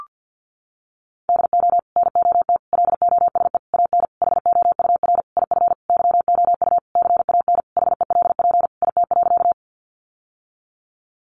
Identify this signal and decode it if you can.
Morse code, heard as a keyed tone, about 50 wpm, 700 Hz. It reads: DO NOT LOSE AN HOUR IF YOU CAN HELP IT.
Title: Sentences from Top 1100 Words 50wpm